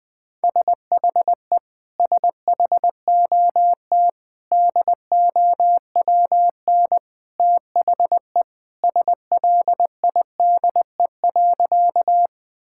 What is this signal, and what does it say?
Morse code, heard as a keyed tone, about 20 wpm, 700 Hz. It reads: SHE SHOT DOWN THE SLIDE.
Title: Sentences from Top 1600 Words 20wpm